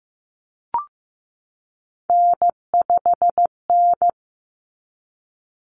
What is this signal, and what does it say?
Morse code, heard as a keyed tone, about 15 wpm, 700 Hz. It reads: N5N
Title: Common Cut Numbers - 3-Digit 15wpm